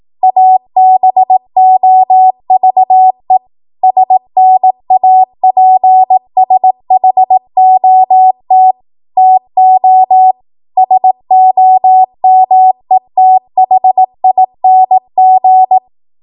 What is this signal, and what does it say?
Morse code, heard as a keyed tone, about 18 wpm, 750 Hz. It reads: ABOVE SNAPSHOT TO SOMETHING
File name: Archive / 18 WPM / 18WPM